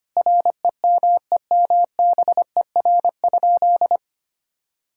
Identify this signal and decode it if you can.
Morse code, heard as a keyed tone, about 25 wpm, 700 Hz. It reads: REMEMBER?
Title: Sentences from Top 1000 Words - Review Part 6 - 25wpm